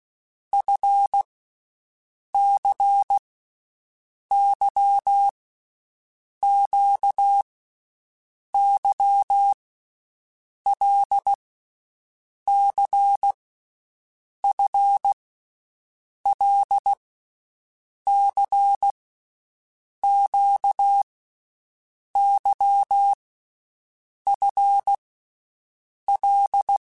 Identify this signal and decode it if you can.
Morse code, heard as a keyed tone, about 16 wpm, 800 Hz. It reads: F C Y Q Y L C F L C Q Y F L
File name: Groups@10 / Groups@10 mp3s / LFCYQ@10